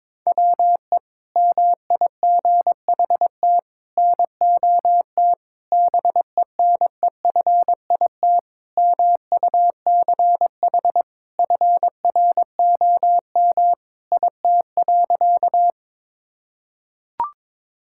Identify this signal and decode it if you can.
Morse code, heard as a keyed tone, about 22 wpm, 700 Hz. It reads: WE MIGHT NOT BENEFIT MUCH FROM IT.
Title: Sentences from Top 1000 Words - Review Part 6 - 22wpm